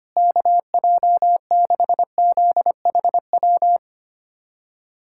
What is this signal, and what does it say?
Morse code, heard as a keyed tone, about 25 wpm, 700 Hz. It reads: KJ6ZHW
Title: USA Callsign DE Callsign - 6 Characters - 25wpm